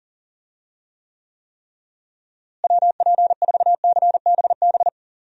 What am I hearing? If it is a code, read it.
Morse code, heard as a keyed tone, about 40 wpm, 700 Hz. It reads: WP4CBB
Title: USA Call Signs - 6 Characters - Repeated 3x - 40wpm